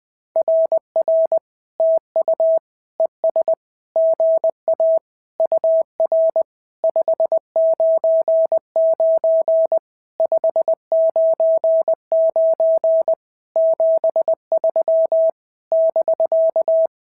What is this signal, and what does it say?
Morse code, heard as a keyed tone, about 20 wpm, 650 Hz. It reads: RR TU ES GA UR 599 599 73 <BK>
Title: SOTA QSO Practice 20wpm